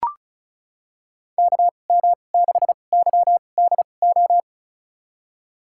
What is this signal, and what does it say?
Morse code, heard as a keyed tone, about 34 wpm, spaced as for 25 wpm, 700 Hz. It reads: KM6YDO
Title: USA Call Signs - 6 Characters - 2x Character Spacing - 35wpm